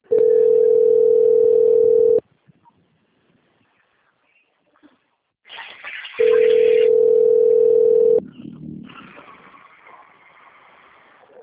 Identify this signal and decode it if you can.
Ringback tone (North America), 440 + 480 Hz.
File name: out- - - - - .14998